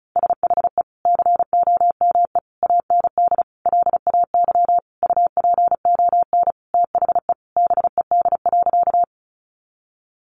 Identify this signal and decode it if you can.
Morse code, heard as a keyed tone, about 34 wpm, 700 Hz. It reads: SHE COME AND LAY UPON THE BED.
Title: Sentences from Top 1100 Words 35wpm